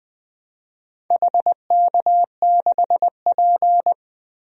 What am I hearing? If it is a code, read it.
Morse code, heard as a keyed tone, about 20 wpm, 700 Hz. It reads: HK6P
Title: DXCC-100 Call Signs - 4 Characters - Repeated 3x - 20wpm